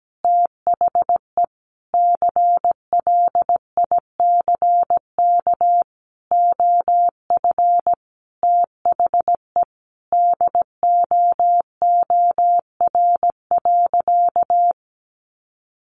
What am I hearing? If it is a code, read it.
Morse code, heard as a keyed tone, about 17 wpm, 700 Hz. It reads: THE CLICK OF THE DOOR.
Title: Sentences from Top 1100 Words 17wpm